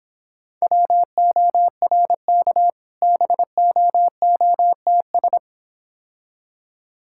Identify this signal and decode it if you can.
Morse code, heard as a keyed tone, about 25 wpm, 700 Hz. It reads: WORK BOOTS